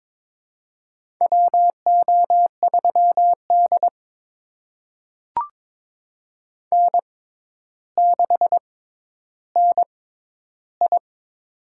Morse code, heard as a keyed tone, about 22 wpm, spaced as for 11 wpm, 700 Hz. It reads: WO3D N6NI
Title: USA Call Signs - 6x Character Spacing - 22wpm